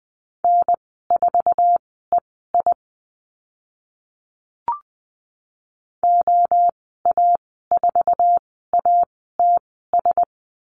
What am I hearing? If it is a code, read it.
Morse code, heard as a keyed tone, about 20 wpm, spaced as for 14 wpm, 700 Hz. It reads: N4EI OA4ATS
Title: 069 Lesson 12g - Letter d - Call Signs 20wpm